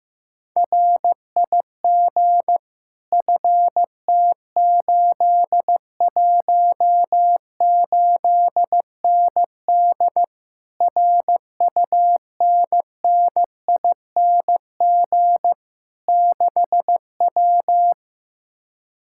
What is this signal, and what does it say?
Morse code, heard as a keyed tone, about 15 wpm, 700 Hz. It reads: RIG FT818ND RUNNING 6W
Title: Speed-Racing Common QSO Phrases Encore - 15wpm